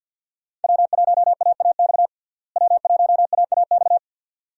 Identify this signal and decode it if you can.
Morse code, heard as a keyed tone, about 52 wpm, 700 Hz. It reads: W1AAX W1AAX
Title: USA Call Signs - 5 Characters - Repeated 2x - 50wpm